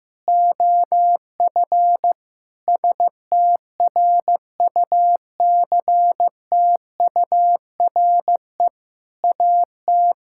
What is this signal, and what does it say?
Morse code, heard as a keyed tone, about 14 wpm, 700 Hz. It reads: OF STRUCTURE AT
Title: Sentences from Top 1000 Words - Review Part 6 - 15wpm